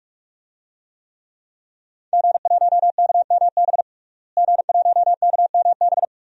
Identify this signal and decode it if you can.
Morse code, heard as a keyed tone, about 46 wpm, 700 Hz. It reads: G1KMB G1KMB
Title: DXCC-100 Call Signs - 5 Characters - Repeated 2x - 45wpm